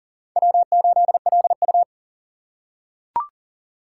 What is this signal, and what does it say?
Morse code, heard as a keyed tone, about 40 wpm, 700 Hz. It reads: W8LU
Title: USA Call Signs - 4 Characters - 3x Character Spacing - 40wpm